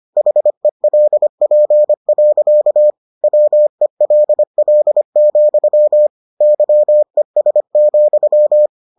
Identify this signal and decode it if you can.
Morse code, heard as a keyed tone, about 25 wpm, 600 Hz. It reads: HELP. WELL, YES,